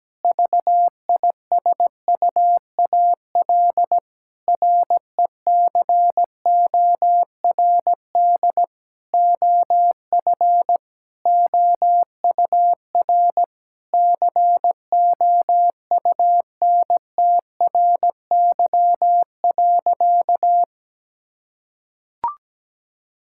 Morse code, heard as a keyed tone, about 17 wpm, 700 Hz. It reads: VISUAL RECORD OF OUR COUNTRY.
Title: Sentences from Top 1000 Words - Review Part 6 - 17wpm